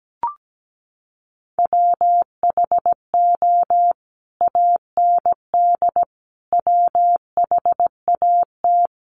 Morse code, heard as a keyed tone, about 17 wpm, 700 Hz. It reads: WHO AND WHAT
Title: Sentences from Top 100 Words 17wpm